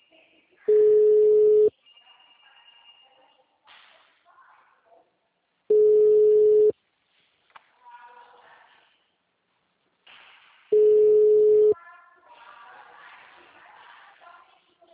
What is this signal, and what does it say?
Ringback tone (Europe, 425 Hz), with speech.
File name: out- -2001- - - .56404